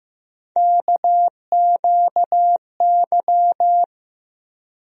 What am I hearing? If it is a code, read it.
Morse code, heard as a keyed tone, about 15 wpm, 700 Hz. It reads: KQY
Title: USA Call Sign Postfixes - 3 Letters 15wpm